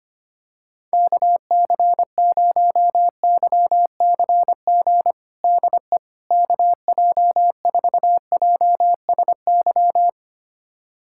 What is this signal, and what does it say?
Morse code, heard as a keyed tone, about 25 wpm, 700 Hz. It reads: KC0YCG DE KJ4JSY
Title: USA Callsign DE Callsign - 6 Characters - 25wpm